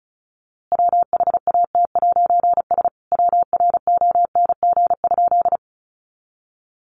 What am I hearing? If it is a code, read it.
Morse code, heard as a keyed tone, about 34 wpm, 700 Hz. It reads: WHAT'S WRONG?
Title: A Sound Of Thunder 35wpm